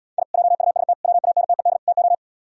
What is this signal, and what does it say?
Morse code, heard as a keyed tone, about 76 wpm, 700 Hz. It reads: I JUST WANTED TO